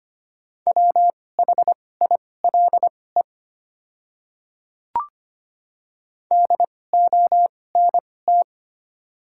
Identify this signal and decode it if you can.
Morse code, heard as a keyed tone, about 25 wpm, spaced as for 18 wpm, 700 Hz. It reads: WHILE DONT
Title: 2x Character Spacing - Top 300 Words - 25wpm